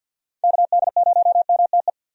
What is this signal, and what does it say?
Morse code, heard as a keyed tone, about 50 wpm, 700 Hz. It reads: KD0MTE
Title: USA Call Signs - 6 Characters - Repeated 2x - 50wpm